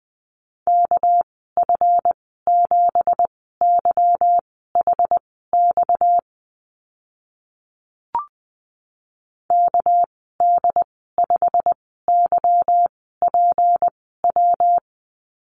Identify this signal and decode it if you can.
Morse code, heard as a keyed tone, about 20 wpm, spaced as for 15 wpm, 700 Hz. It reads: KF7YHX KD5YPW
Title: 245 Lesson 39g - Letter z - Call Signs 20wpm